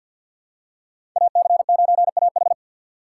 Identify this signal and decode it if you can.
Morse code, heard as a keyed tone, about 53 wpm, 700 Hz. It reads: AC9AH
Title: USA Call Signs - 5 Characters - 2x Character Spacing - 50wpm